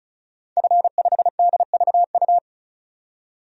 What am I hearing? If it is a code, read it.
Morse code, heard as a keyed tone, about 36 wpm, 700 Hz. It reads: F5DVU
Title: DXCC-100 Call Signs - 5 Characters - 35wpm